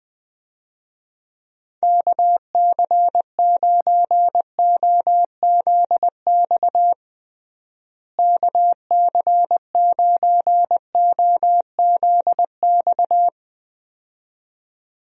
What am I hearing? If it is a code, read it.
Morse code, heard as a keyed tone, about 20 wpm, 700 Hz. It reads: KC9OZX KC9OZX
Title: USA Call Signs - 6 Characters - Repeated 2x - 20wpm